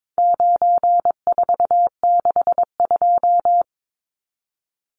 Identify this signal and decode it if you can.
Morse code, heard as a keyed tone, about 22 wpm, 700 Hz. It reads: 9462
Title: Numbers - 4-Digit 22wpm